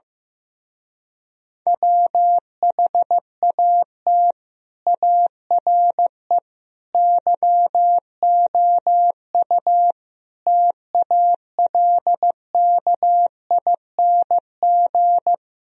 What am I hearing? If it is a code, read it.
Morse code, heard as a keyed tone, about 15 wpm, 700 Hz. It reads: WHAT ARE YOU TALKING
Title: Speed-Racing - Sets of 5 Words 15wpm